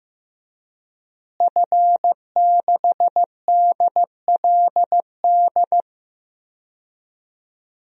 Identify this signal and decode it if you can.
Morse code, heard as a keyed tone, about 15 wpm, 700 Hz. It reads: F6DLD
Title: DXCC-100 Call Signs - 5 Characters - Repeated 3x - 15wpm